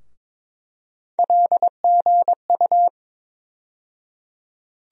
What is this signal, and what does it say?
Morse code, heard as a keyed tone, about 22 wpm, 700 Hz. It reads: LGU
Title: USA Call Sign Postfixes - 3 Letters 22wpm